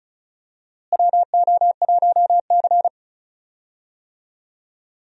Morse code, heard as a keyed tone, about 36 wpm, 700 Hz. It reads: WO1C
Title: USA Call Signs - 4 Characters - Repeated 3x - 35wpm